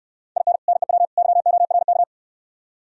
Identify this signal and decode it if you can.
Morse code, heard as a keyed tone, about 67 wpm, 700 Hz. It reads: IN NEW YORK